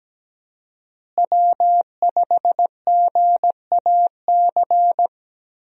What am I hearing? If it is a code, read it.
Morse code, heard as a keyed tone, about 17 wpm, 700 Hz. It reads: W5GAC
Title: USA Call Signs - 5 Characters - Repeated 2x - 17wpm